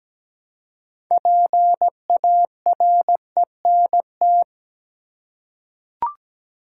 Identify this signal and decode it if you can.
Morse code, heard as a keyed tone, about 17 wpm, 700 Hz. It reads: PARENT
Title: Six Letter Words - 17wpm